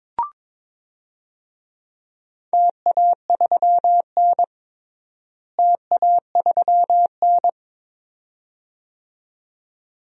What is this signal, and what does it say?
Morse code, heard as a keyed tone, about 22 wpm, 700 Hz. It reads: TA3N TA3N
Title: DXCC-100 Call Signs - 4 Characters - Repeated 2x - 22wpm